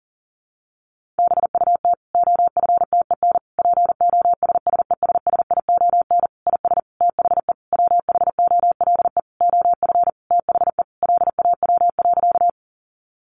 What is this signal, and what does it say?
Morse code, heard as a keyed tone, about 40 wpm, 700 Hz. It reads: BUT OFTEN POSSESSION IS THE WHOLE OF THE LAW.
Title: Sentences from Top 1400 Words 40wpm